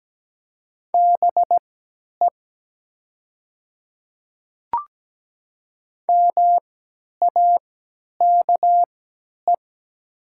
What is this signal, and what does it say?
Morse code, heard as a keyed tone, about 17 wpm, spaced as for 10 wpm, 700 Hz. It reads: BE MAKE
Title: 3x Character Spacing - Top 300 Words - 17wpm